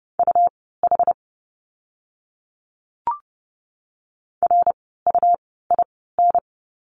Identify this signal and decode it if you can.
Morse code, heard as a keyed tone, about 30 wpm, spaced as for 17 wpm, 700 Hz. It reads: UH RUIN
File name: Lesson 17f - Letter u - Words 30wpm